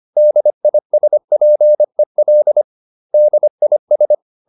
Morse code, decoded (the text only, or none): DISPEL DIS